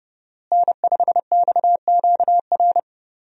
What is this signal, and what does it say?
Morse code, heard as a keyed tone, about 30 wpm, 700 Hz. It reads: N5XQR
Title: USA Call Signs - 5 Characters - Repeated 2x - 30wpm